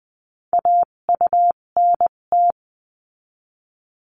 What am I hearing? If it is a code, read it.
Morse code, heard as a keyed tone, about 20 wpm, spaced as for 17 wpm, 700 Hz. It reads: AUNT
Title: 257 Lesson 42b - Words 14-20wpm